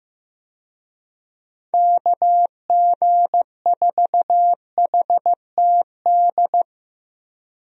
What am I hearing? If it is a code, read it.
Morse code, heard as a keyed tone, about 15 wpm, 700 Hz. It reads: KG4HTD
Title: USA Call Signs - 6 Characters - Repeated 2x - 15wpm